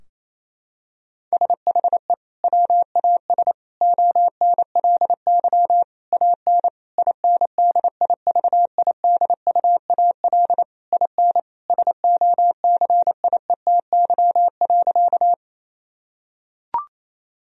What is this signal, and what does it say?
Morse code, heard as a keyed tone, about 28 wpm, 700 Hz. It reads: SHE WAS ONLY AN INDIVIDUAL IN SOCIETY.
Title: Sentences from Top 1000 Words - Review Part 4 - 28wpm